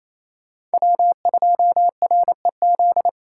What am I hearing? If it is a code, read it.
Morse code, heard as a keyed tone, about 35 wpm, 700 Hz. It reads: W2REZ
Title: Speed Racing - USA 1x3 Call Signs 28wpm